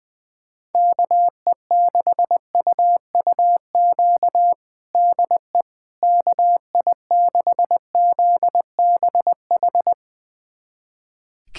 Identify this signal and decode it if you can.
Morse code, heard as a keyed tone, about 21 wpm, 700 Hz. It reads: KE6UUQ DE KI6ZBH
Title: USA Callsign DE Callsign 20wpm